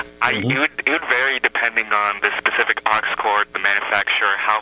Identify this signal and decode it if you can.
Emergency Alert System (SAME) tones, heard as harsh buzzing bursts, with speech. It is heard from 1.0 to 4.0 s.
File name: mp3files / off the hook overtime -128